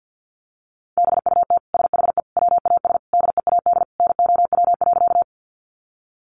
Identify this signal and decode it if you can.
Morse code, heard as a keyed tone, about 50 wpm, 700 Hz. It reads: BUT SHE WAS DEAD NOW.
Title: Sentences from Top 1500 Words 50wpm